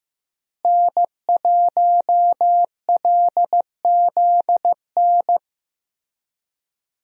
Morse code, heard as a keyed tone, about 15 wpm, 700 Hz. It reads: N1LZN
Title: USA Call Signs - 5 Characters - Repeated 3x - 15wpm